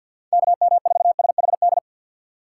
Morse code, heard as a keyed tone, about 46 wpm, 690 Hz. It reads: KM4SHD
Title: USA Call Signs - 2x Character Spacing - 50wpm